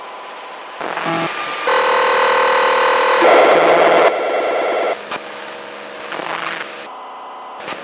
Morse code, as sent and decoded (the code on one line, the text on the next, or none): none